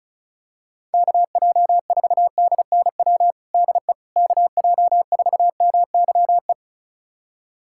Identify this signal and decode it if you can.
Morse code, heard as a keyed tone, about 34 wpm, 700 Hz. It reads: KJ4DNW DE KJ4MYE
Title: USA Callsign DE Callsign - 6 Characters - 35wpm